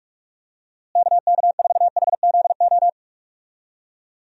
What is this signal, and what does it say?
Morse code, heard as a keyed tone, about 46 wpm, 700 Hz. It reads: KK4HZO
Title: USA Call Signs - 6 Characters - Repeated 2x - 45wpm